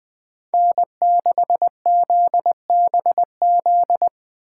Morse code, heard as a keyed tone, about 20 wpm, 700 Hz. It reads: N6ZBZ
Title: Speed Racing - USA 1x3 Call Signs 17wpm